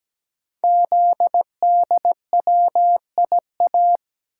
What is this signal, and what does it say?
Morse code, heard as a keyed tone, about 17 wpm, 700 Hz. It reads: ZDWIA
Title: Code Groups - 5 Letters - 17wpm